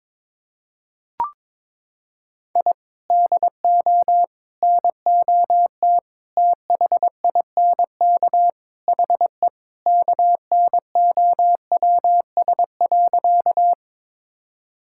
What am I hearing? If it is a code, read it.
Morse code, heard as a keyed tone, about 22 wpm, 700 Hz. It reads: I DO NOT THINK HE KNOWS.
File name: practice / 22wpm / Sentences with 4-Word Phrases - 22wpm